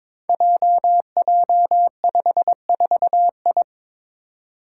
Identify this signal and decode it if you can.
Morse code, heard as a keyed tone, about 22 wpm, 700 Hz. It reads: JJ54I
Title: Five Challenging Letter-Number Combinations 22wpm